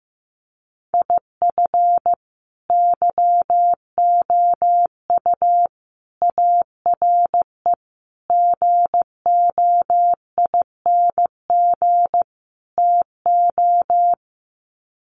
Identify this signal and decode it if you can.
Morse code, heard as a keyed tone, about 15 wpm, 700 Hz. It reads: IF YOU ARE GOING TO
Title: Sets of 5 Words Encore2 15wpm